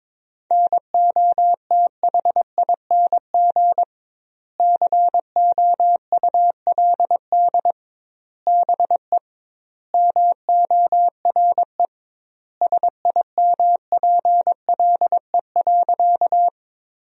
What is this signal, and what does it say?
Morse code, heard as a keyed tone, about 22 wpm, 700 Hz. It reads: NOTHING COULD BE MORE SIMPLE.
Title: 2x Word-Spacing - Sentences from Top 500 Words - 22wpm